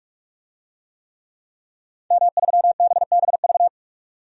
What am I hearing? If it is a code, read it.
Morse code, heard as a keyed tone, about 46 wpm, 700 Hz. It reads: M3BBV
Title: DXCC-100 Call Signs - 5 Characters - 45wpm